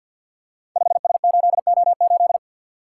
Morse code, heard as a keyed tone, about 46 wpm, 700 Hz. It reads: 5S8O8
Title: Code Groups - 5 Letter:Numbers - 50wpm